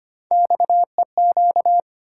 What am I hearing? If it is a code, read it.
Morse code, heard as a keyed tone, about 25 wpm, 700 Hz. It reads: XEQ
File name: Three Letter-Number Combinations - 25wpm